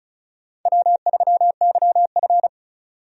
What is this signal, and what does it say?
Morse code, heard as a keyed tone, about 36 wpm, 700 Hz. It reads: W3YF
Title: USA Call Signs - 4 Characters - 35wpm